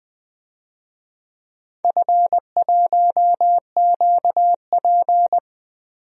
Morse code, heard as a keyed tone, about 20 wpm, 700 Hz. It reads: F1QP
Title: DXCC-100 Call Signs - 4 Characters - Repeated 2x - 20wpm